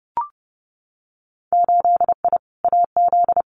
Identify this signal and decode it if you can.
Morse code, heard as a keyed tone, about 30 wpm, 700 Hz. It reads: 8I AZ
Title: Winter Field Day Exchanges Encore 30wpm